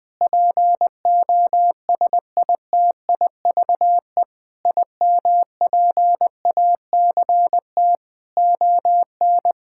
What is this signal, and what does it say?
Morse code, heard as a keyed tone, about 20 wpm, 700 Hz. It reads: POSITIVE IMPACT ON
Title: Sentences from Top 1000 Words - Review Part 6 - 20wpm